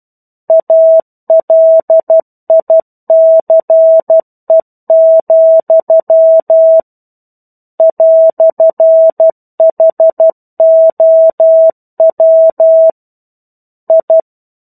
Morse code, heard as a keyed tone, about 12 wpm, 650 Hz. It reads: ALICE, "HOW I
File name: 01-Chapter-01 12 WPM